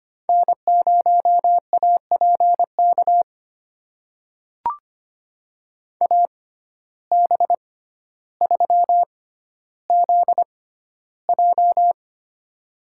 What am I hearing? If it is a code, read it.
Morse code, heard as a keyed tone, about 25 wpm, spaced as for 13 wpm, 700 Hz. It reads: N0APK AB3ZJ
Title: USA Call Signs - 5 Characters - 6x Character Spacing - 25wpm